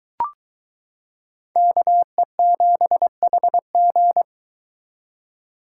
Morse code, heard as a keyed tone, about 22 wpm, 700 Hz. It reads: KE7HG